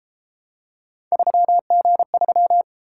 Morse code, heard as a keyed tone, about 32 wpm, 700 Hz. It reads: 3G3